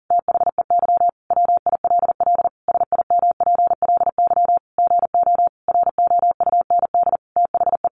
Morse code, heard as a keyed tone, about 40 wpm, 700 Hz. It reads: THEY WILL SIMPLY GO ROUND THE
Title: Sentences from Top 600 Words 40wpm